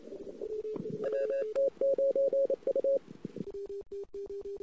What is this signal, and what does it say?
Morse code, heard as a keyed tone, about 28 wpm, 550 Hz. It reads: WT9U
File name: CW / 2008-ss-cw-k5tr-op-n5rz -22:00